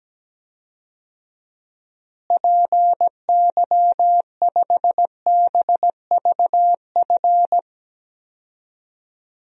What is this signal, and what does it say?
Morse code, heard as a keyed tone, about 17 wpm, 700 Hz. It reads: PY5BVF
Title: DXCC-100 Call Signs - 6 Characters - Repeated 3x - 17wpm